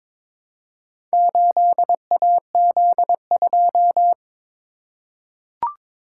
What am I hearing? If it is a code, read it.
Morse code, heard as a keyed tone, about 23 wpm, 700 Hz. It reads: 8AZ2
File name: practice / 22wpm / Four Letter-Number Combinations - 22wpm